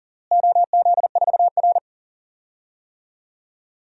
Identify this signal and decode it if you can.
Morse code, heard as a keyed tone, about 40 wpm, 700 Hz. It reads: OZ4R
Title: DXCC-100 Call Signs - 4 Characters - Repeated 3x - 40wpm